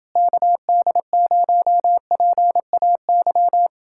Morse code, heard as a keyed tone, about 29 wpm, 700 Hz. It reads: KD0PAY